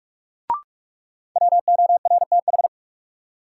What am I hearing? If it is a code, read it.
Morse code, heard as a keyed tone, about 46 wpm, 700 Hz. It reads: WORTH
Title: Sentences from Top 800 Words - New Word Drill - 45wpm